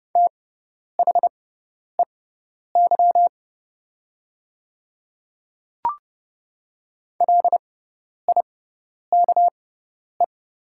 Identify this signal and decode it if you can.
Morse code, heard as a keyed tone, about 30 wpm, spaced as for 10 wpm, 700 Hz. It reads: THEY LIKE
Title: 6x Character Spacing - Top 100 Words - 30wpm